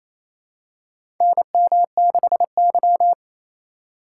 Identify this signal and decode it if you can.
Morse code, heard as a keyed tone, about 28 wpm, 700 Hz. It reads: NM6Y
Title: USA Call Signs - 4 Characters - 3x Character Spacing - 28wpm